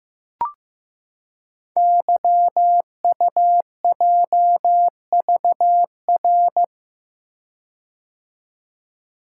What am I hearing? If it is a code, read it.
Morse code, heard as a keyed tone, about 15 wpm, 700 Hz. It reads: YUJVR
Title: Code Groups - 5 Letters - 15wpm